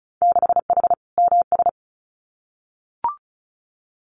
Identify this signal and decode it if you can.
Morse code, heard as a keyed tone, about 36 wpm, 700 Hz. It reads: 6H MS
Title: Winter Field Day Exchanges Encore 35wpm